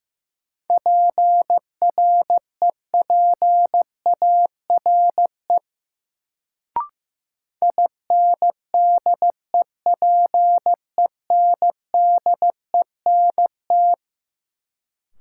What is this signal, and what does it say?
Morse code, heard as a keyed tone, about 15 wpm, 700 Hz. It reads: PREPARE INDEPENDENT
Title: Sentences from Top 900 Words - New Word Drill - 15wpm